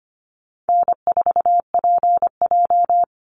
Morse code, heard as a keyed tone, about 25 wpm, 700 Hz. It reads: N4PJ
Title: Call Signs Repeated 3x - 25wpm